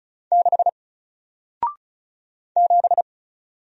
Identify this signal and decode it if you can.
Morse code, heard as a keyed tone, about 36 wpm, 700 Hz. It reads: 6 7
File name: Single-Letter-Number-35wpm